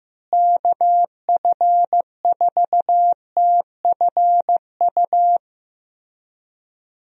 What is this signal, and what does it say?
Morse code, heard as a keyed tone, about 15 wpm, 700 Hz. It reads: KF4TFU
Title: USA Call Signs - 6 Characters - Repeated 3x - 15wpm